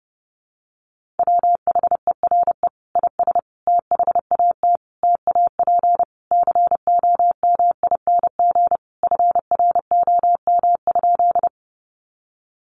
Morse code, heard as a keyed tone, about 30 wpm, 700 Hz. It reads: WHERE IS THAT TAP COMING FROM?
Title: Sentences from Top 1400 Words 30wpm